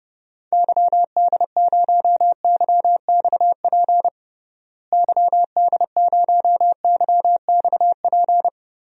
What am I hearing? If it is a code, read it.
Morse code, heard as a keyed tone, about 32 wpm, 700 Hz. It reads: YD0YXP YD0YXP
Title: DXCC-100 Call Signs - 6 Characters - Repeated 2x - 30wpm